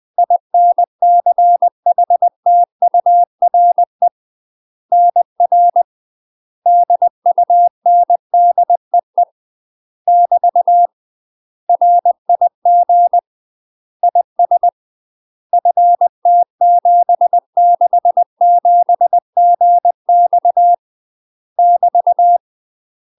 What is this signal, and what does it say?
Morse code, heard as a keyed tone, about 20 wpm, 700 Hz. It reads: INCHTURE NR DUNDEE <BT> RIG IS FT767GX <BT>
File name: UK MASTER QSOs TRAINING / 20WPM / UK MASTER QSOs 0044